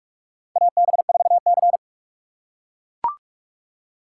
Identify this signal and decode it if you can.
Morse code, heard as a keyed tone, about 43 wpm, 700 Hz. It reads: AB4C
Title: USA Call Signs - 4 Characters - 4x Character Spacing - 45wpm